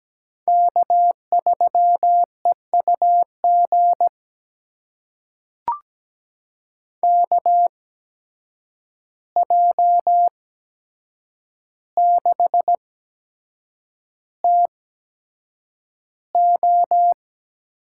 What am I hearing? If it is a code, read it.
Morse code, heard as a keyed tone, about 17 wpm, spaced as for 7 wpm, 700 Hz. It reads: K3EUG KJ6TO
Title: USA Call Signs - 5 Characters - 8x Character Spacing - 17wpm